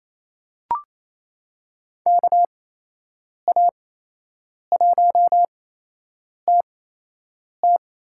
Morse code, heard as a keyed tone, about 27 wpm, 700 Hz. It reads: K A 1 T T
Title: USA Call Signs - 6 Characters - 8x Character Spacing - 28wpm